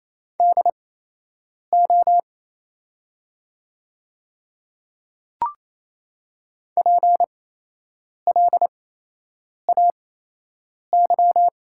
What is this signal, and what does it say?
Morse code, heard as a keyed tone, about 28 wpm, spaced as for 8 wpm, 700 Hz. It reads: DO PLAY
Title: 8x Character Spacing - Top 200 Words - 28wpm